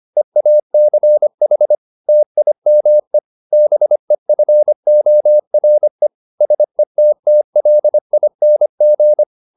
Morse code, heard as a keyed tone, about 25 wpm, 600 Hz. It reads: EACH TIME BEFORE SETTLING